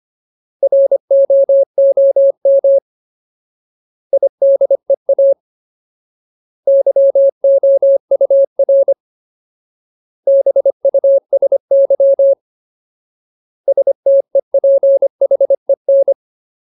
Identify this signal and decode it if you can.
Morse code, heard as a keyed tone, about 25 wpm, 550 Hz. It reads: ROOM IDEA YOUR BUSY STEPHEN
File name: Ultimate Word List / 25WPM Ultimate